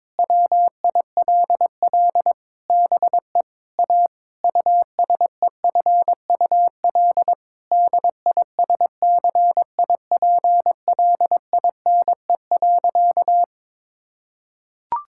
Morse code, heard as a keyed tone, about 22 wpm, 700 Hz. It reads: WILL BE A USEFUL DISCIPLINE.
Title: Sentences from Top 1000 Words - Review Part 6 - 22wpm